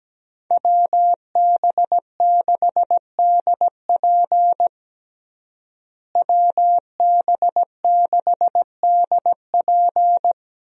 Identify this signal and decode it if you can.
Morse code, heard as a keyed tone, about 17 wpm, 700 Hz. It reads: WB6DP WB6DP
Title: USA Call Signs - 5 Characters - Repeated 2x - 17wpm